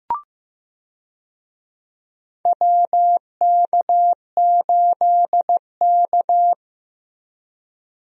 Morse code, heard as a keyed tone, about 15 wpm, 700 Hz. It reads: WK8K
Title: USA Call Signs - 4 Characters - Repeated 2x - 15wpm